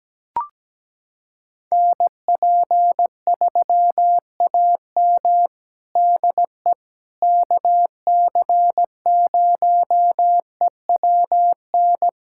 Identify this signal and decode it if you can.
Morse code, heard as a keyed tone, about 17 wpm, 700 Hz. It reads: NP3AM DE KC0EWN
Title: USA Callsign DE Callsign 17wpm